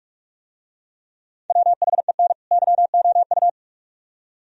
Morse code, heard as a keyed tone, about 44 wpm, 700 Hz. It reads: WHEN YOU